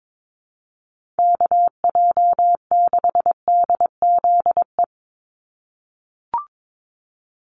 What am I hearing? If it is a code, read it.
Morse code, heard as a keyed tone, about 22 wpm, 700 Hz. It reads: KJ6DZE
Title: USA Call Signs 22wpm